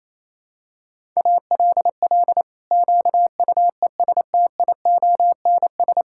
Morse code, heard as a keyed tone, about 28 wpm, 700 Hz. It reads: ALL QUESTIONS
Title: Pangram Challenge v2 - 28wpm